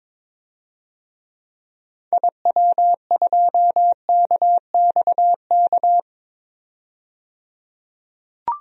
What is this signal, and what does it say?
Morse code, heard as a keyed tone, about 22 wpm, 700 Hz. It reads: IW2KXK
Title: DXCC-100 Call Signs - 6 Characters - 22wpm